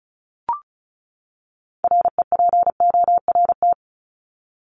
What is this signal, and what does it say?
Morse code, heard as a keyed tone, about 34 wpm, 700 Hz. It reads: REPORT
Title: ED Suffix - Top 100 Words with Base Words 35wpm